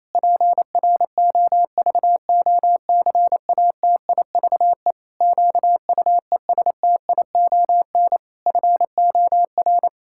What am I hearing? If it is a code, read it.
Morse code, heard as a keyed tone, about 28 wpm, 700 Hz. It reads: PROVOCATIVE QUESTION FOR
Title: Pangram Challenge v2 - 28wpm